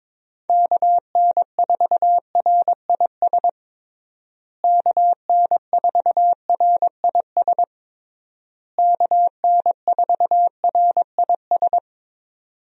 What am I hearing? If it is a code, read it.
Morse code, heard as a keyed tone, about 22 wpm, 700 Hz. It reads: KN4RIS KN4RIS KN4RIS
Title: USA Call Signs - 6 Characters - Repeated 3x - 22wpm